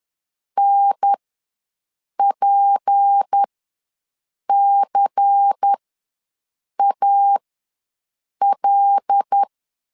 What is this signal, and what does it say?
Morse code, heard as a keyed tone, about 10 wpm, spaced as for 6 wpm, 800 Hz. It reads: NPCAL
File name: LEZIONI / 6 PKX0 / 6 P ws 10wpm 3751